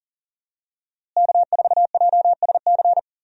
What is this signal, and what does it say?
Morse code, heard as a keyed tone, about 38 wpm, 700 Hz. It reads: K4JSC
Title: USA Call Signs - 5 Characters - Repeated 3x - 40wpm